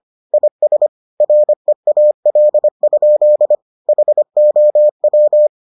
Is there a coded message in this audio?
Yes: Morse code from 0.3 to 5.5 s, heard as a keyed tone, about 25 wpm, 600 Hz. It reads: IS REAL? HOW